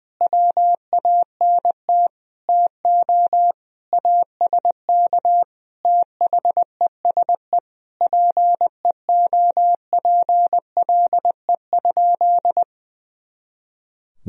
Morse code, heard as a keyed tone, about 21 wpm, 700 Hz. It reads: WANT TO ASK THESE PEOPLE?
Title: Sentences from Top 200 Words and Abbreviations 20wpm